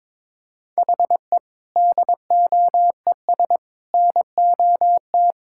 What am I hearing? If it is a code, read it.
Morse code, heard as a keyed tone, about 22 wpm, 700 Hz. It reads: HE DOES NOT
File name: practice / 22wpm / Sentences with 3-Word Phrases - 22wpm